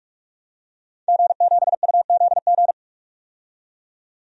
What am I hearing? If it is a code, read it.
Morse code, heard as a keyed tone, about 47 wpm, 700 Hz. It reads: G7UZG
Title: DXCC-100 Call Signs - 5 Characters - 45wpm